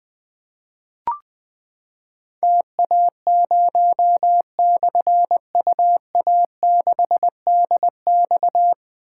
Morse code, heard as a keyed tone, about 20 wpm, 700 Hz. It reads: TA0/UA6DX
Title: Difficult Call Signs 20wpm